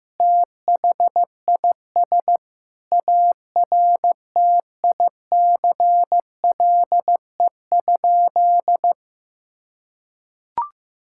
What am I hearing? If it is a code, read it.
Morse code, heard as a keyed tone, about 15 wpm, 700 Hz. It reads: THIS ARTICLE?
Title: Sentences from Top 1000 Words - Review Part 5 - 15wpm